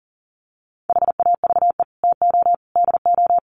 Morse code, heard as a keyed tone, about 40 wpm, 700 Hz. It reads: HAVE TO DO
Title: Sets of 3 Words 40wpm